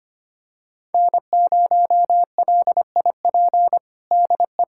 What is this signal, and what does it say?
Morse code, heard as a keyed tone, about 25 wpm, 700 Hz. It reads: N0LIP DE
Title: USA Callsign DE Callsign - 5 Characters - 25wpm